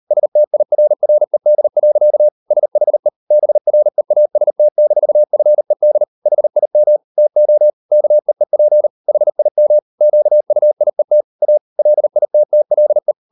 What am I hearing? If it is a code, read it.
Morse code, heard as a keyed tone, about 38 wpm, 600 Hz. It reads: STIRRED. SHE BREAST-FED HIM TO KEEP HIM QUIET A LITTLE